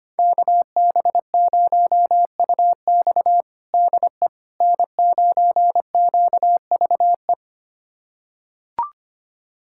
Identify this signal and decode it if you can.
Morse code, heard as a keyed tone, about 25 wpm, 700 Hz. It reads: KB0UX DE N9QVE
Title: USA Callsign DE Callsign 25wpm